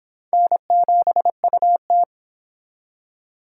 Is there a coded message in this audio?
Yes: Morse code from 0.3 to 2.0 s, heard as a keyed tone, about 25 wpm, 700 Hz. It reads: N7UT